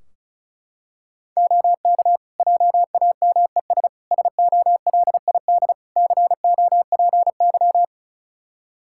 Morse code, heard as a keyed tone, about 36 wpm, 700 Hz. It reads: OK JAMES SOLID COPY
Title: Practice QSO - Exchange 3 - More Info - 35wpm